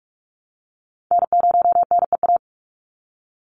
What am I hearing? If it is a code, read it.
Morse code, heard as a keyed tone, about 44 wpm, 700 Hz. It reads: N0NEA
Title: USA Call Signs 45wpm